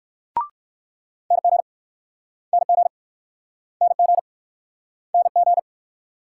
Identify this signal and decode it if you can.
Morse code, heard as a keyed tone, about 58 wpm, 700 Hz. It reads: NG NG NG NG